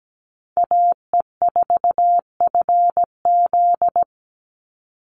Morse code, heard as a keyed tone, about 17 wpm, 700 Hz. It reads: AE4FZ
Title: NA SOTA Chaser Callsigns 17wpm